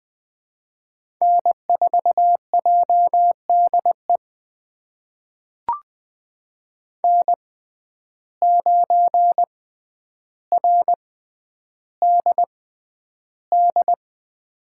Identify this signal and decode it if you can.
Morse code, heard as a keyed tone, about 21 wpm, spaced as for 10 wpm, 700 Hz. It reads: N4JDE N9RDD
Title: USA Call Signs - 5 Characters - 6x Character Spacing - 20wpm